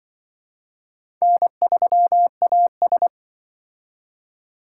Morse code, heard as a keyed tone, about 23 wpm, 700 Hz. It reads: N3AS
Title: Speed Racing - USA 1x2 Call Signs 20wpm